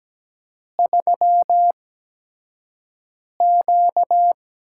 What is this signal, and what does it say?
Morse code, heard as a keyed tone, about 17 wpm, spaced as for 5 wpm, 700 Hz. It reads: 3Q
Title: USA Call Signs - 4 Characters - 8x Character Spacing - 17wpm